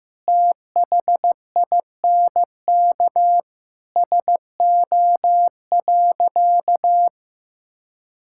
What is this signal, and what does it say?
Morse code, heard as a keyed tone, about 15 wpm, 700 Hz. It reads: THINK SO.
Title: Sentences from Top 1000 Words - Review Part 5 - 15wpm